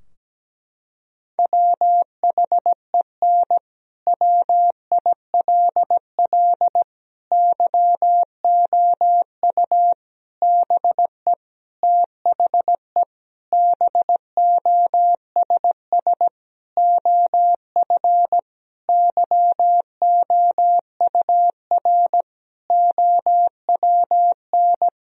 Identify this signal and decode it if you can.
Morse code, heard as a keyed tone, about 17 wpm, 700 Hz. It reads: WHEN WILL YOU BE THE BOSS OF YOUR OWN
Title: Sentences from Top 1000 Words - Review Part 4 - 17wpm